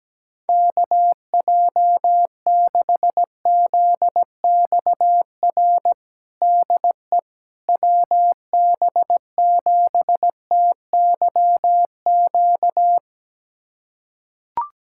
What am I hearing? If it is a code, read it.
Morse code, heard as a keyed tone, about 17 wpm, 700 Hz. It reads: KJ6ZXR DE WB7TYQ
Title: USA Callsign DE Callsign - 6 Characters - 17wpm